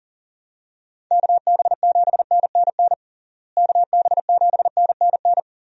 Morse code, heard as a keyed tone, about 40 wpm, 700 Hz. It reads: KB7NNN KB7NNN
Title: USA Call Signs - 6 Characters - Repeated 2x - 40wpm